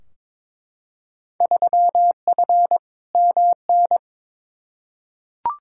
Morse code, heard as a keyed tone, about 22 wpm, 700 Hz. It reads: 3F MN
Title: Field Day Exchanges Encore - 22wpm